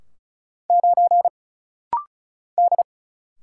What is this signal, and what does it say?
Morse code, heard as a keyed tone, about 34 wpm, 700 Hz. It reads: 9 D
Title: Single Letter-Number - Rapid-Fire 35wpm